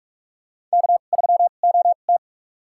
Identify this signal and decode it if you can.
Morse code, heard as a keyed tone, about 47 wpm, spaced as for 33 wpm, 700 Hz. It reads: K3OT
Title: USA Call Signs - 4 Characters - 2x Character Spacing - 45wpm